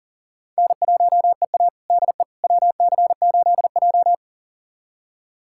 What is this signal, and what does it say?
Morse code, heard as a keyed tone, about 40 wpm, 700 Hz. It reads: N1EA DE WC8J
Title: USA Callsign DE Callsign - 4 Characters - 40wpm